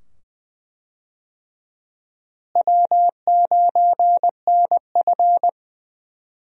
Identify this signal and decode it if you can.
Morse code, heard as a keyed tone, about 21 wpm, 700 Hz. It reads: W9NF
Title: USA Call Signs - 4 Characters - Repeated 2x - 20wpm